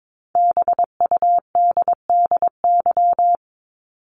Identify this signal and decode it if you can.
Morse code, heard as a keyed tone, about 22 wpm, 700 Hz. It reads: BUDDY
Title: Top 1751-2000 Words 22wpm